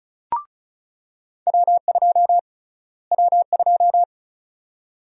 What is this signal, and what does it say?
Morse code, heard as a keyed tone, about 36 wpm, 700 Hz. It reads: W2 W2
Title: USA Call Sign Prefixes - Repeated 2x - 35wpm